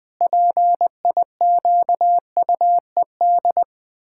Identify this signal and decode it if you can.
Morse code, heard as a keyed tone, about 20 wpm, 700 Hz. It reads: PIQUED
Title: Pangram Challenge v2 - 20wpm